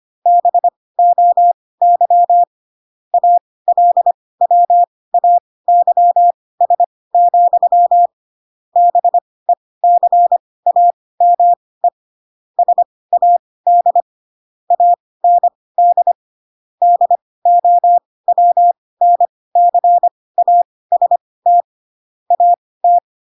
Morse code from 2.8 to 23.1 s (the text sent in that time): ALWAYS, BECAME SAD AND DOWNCAST AT